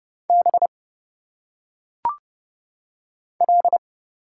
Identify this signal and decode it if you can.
Morse code, heard as a keyed tone, about 30 wpm, 700 Hz. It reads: B L
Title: Single Letters - Phonetic Pronunciation 30wpm